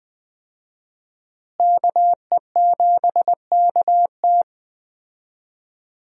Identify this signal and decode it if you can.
Morse code, heard as a keyed tone, about 21 wpm, 700 Hz. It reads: KE7KT
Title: USA Call Signs - 5 Characters - Repeated 3x - 20wpm